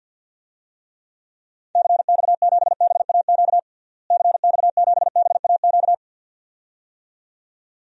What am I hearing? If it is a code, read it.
Morse code, heard as a keyed tone, about 51 wpm, 700 Hz. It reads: CX7BAQ CX7BAQ
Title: DXCC-100 Call Signs - 6 Characters - Repeated 2x - 50wpm